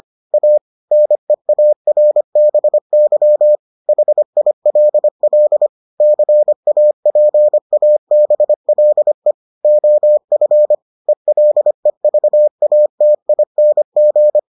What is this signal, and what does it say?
Morse code, heard as a keyed tone, about 25 wpm, 600 Hz. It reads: A NEARBY HILL CAPABLE OF ELEVATING